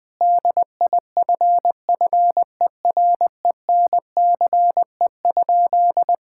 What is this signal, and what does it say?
Morse code, heard as a keyed tone, about 20 wpm, 700 Hz. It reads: DIFFERENCE?
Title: Sentences from Top 1000 Words - Review Part 5 - 20wpm